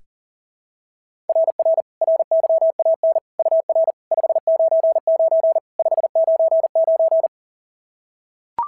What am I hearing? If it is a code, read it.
Morse code, heard as a keyed tone, about 40 wpm, 690 Hz. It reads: RR RYAN UR 599 599
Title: SOTA QSO Practice 40wpm